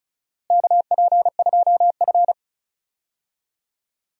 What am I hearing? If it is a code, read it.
Morse code, heard as a keyed tone, about 36 wpm, 690 Hz. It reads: KP2F
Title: DXCC-100 Call Signs - 4 Characters - Repeated 3x - 35wpm